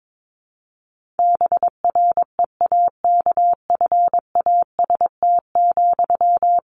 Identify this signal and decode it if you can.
Morse code, heard as a keyed tone, about 22 wpm, 700 Hz. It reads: BREAKFAST,
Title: Sentences from Top 1500 Words 22wpm